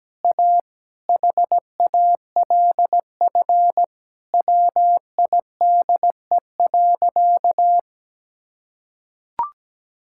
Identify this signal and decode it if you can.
Morse code, heard as a keyed tone, about 17 wpm, 700 Hz. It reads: A HALF WIDE.
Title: Sentences from Top 1000 Words - Review Part 6 - 17wpm